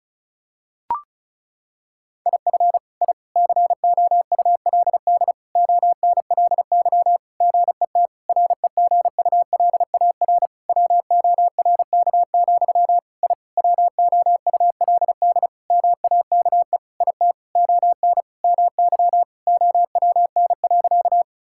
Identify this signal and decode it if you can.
Morse code, heard as a keyed tone, about 34 wpm, 700 Hz. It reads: IF I COULD ONLY GET REGULAR WORK, I WOULD MAKE IT ON MY OWN.
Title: Sentences from Top 1000 Words - Review Part 6 - 35wpm